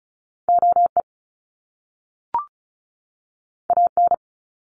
Morse code, heard as a keyed tone, about 34 wpm, 700 Hz. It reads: OE AN
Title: Top Two Letter Combinations in Words 35wpm